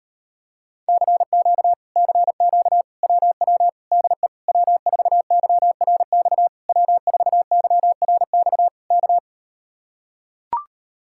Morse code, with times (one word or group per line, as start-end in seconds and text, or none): 0.9-1.7 CQ
2.0-2.8 CQ
3.0-3.7 WW
3.9-4.3 DE
4.5-6.5 W4YRX
6.7-8.7 W4YRX
8.9-9.2 K